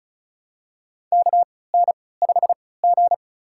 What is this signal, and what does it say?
Morse code, heard as a keyed tone, about 34 wpm, spaced as for 20 wpm, 700 Hz. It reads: KN5G
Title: USA Call Signs - 4 Characters - 3x Character Spacing - 35wpm